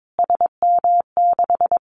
Morse code, heard as a keyed tone, about 22 wpm, 700 Hz. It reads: SM6
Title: Callsign Prefixes for Top 25 DXCC - Repeated 3x - 22wpm